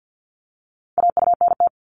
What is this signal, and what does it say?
Morse code, heard as a keyed tone, about 50 wpm, 700 Hz. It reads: AUNT
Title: Multi-letter Phonemes Repeated 3x 50wpm